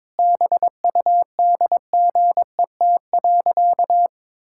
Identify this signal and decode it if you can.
Morse code, heard as a keyed tone, about 22 wpm, 700 Hz. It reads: BUDGET.
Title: Pangram Challenge v2 - 22wpm